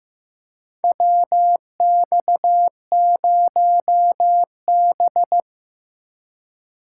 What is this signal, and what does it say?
Morse code, heard as a keyed tone, about 15 wpm, 700 Hz. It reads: WX0B
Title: USA Call Signs - 8x Character Spacing - 15wpm